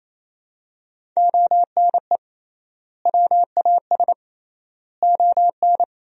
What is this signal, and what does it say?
Morse code, heard as a keyed tone, about 28 wpm, 700 Hz. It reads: ONE WAS ON
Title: 3x Word-Spacing - Sentences from Top 300 Words - 28wpm